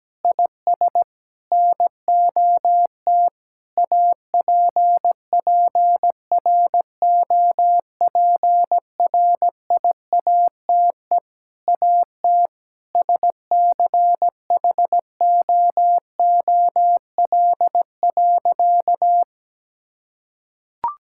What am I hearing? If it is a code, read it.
Morse code, heard as a keyed tone, about 17 wpm, 700 Hz. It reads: IS NOT APPROPRIATE AT SCHOOL.
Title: Sentences from Top 1000 Words - Review Part 5 - 17wpm